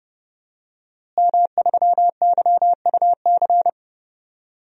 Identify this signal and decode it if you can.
Morse code, heard as a keyed tone, about 30 wpm, 700 Hz. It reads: M3YUC
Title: DXCC-100 Call Signs - 5 Characters - Repeated 3x - 30wpm